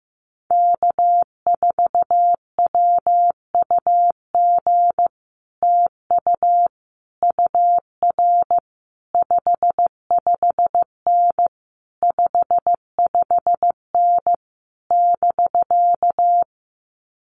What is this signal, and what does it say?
Morse code, heard as a keyed tone, about 15 wpm, 700 Hz. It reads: K4WUG TU UR 55N 55N <BK>
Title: Common QSO Phrases Encore 15wpm